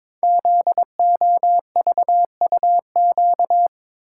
Morse code, heard as a keyed tone, about 22 wpm, 700 Hz. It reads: ZOVUQ
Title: Code Groups - 5 Letters - 22wpm